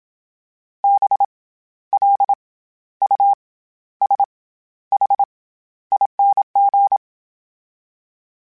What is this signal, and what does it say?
Morse code, heard as a keyed tone, about 27 wpm, 800 Hz. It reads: B L U S H ING
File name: Special INGs-mp3s / Special INGs-mp3s / SlowToFaster ing @25